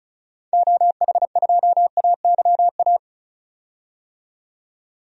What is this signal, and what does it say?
Morse code, heard as a keyed tone, about 36 wpm, 700 Hz. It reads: OH2AYA
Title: DXCC-100 Call Signs - 6 Characters - 35wpm